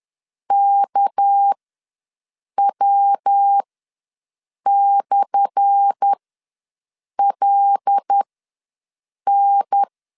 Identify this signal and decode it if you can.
Morse code, heard as a keyed tone, about 10 wpm, spaced as for 6 wpm, 800 Hz. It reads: KW/LN
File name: LEZIONI / 10 SIGNS / 10A SLASH ws 10wpm 4121